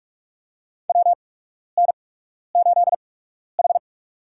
Morse code, heard as a keyed tone, about 46 wpm, spaced as for 12 wpm, 700 Hz. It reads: WN8H
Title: USA Call Signs - 4 Characters - 8x Character Spacing - 45wpm